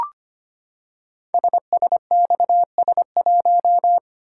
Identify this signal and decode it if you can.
Morse code, heard as a keyed tone, about 25 wpm, 700 Hz. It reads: SSXS1
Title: Code Groups - 5 Letter:Numbers - 25wpm